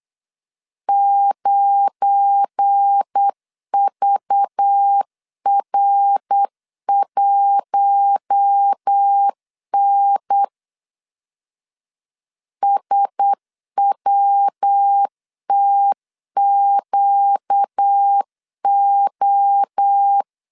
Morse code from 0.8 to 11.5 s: 9VR1N